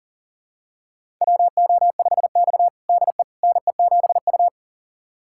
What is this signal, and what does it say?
Morse code, heard as a keyed tone, about 40 wpm, 700 Hz. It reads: WO5X DE NE7U
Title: USA Callsign DE Callsign - 4 Characters - 40wpm